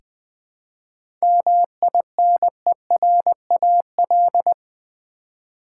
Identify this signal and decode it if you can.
Morse code, heard as a keyed tone, about 20 wpm, 700 Hz. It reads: MINERAL
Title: 7 Letter Words - First 3 Letters Given - 20wpm